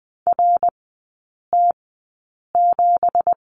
Morse code, heard as a keyed tone, about 20 wpm, spaced as for 8 wpm, 690 Hz. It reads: RT7
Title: 166 Lesson 27e - Number 7 - Three Characters 20wpm